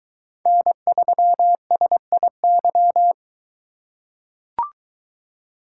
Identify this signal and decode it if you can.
Morse code, heard as a keyed tone, about 23 wpm, 700 Hz. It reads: N3SIY